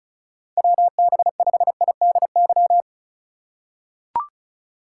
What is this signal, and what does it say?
Morse code, heard as a keyed tone, about 36 wpm, 700 Hz. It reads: WB5IDY
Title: K1USN SST QSO Practice - 35wpm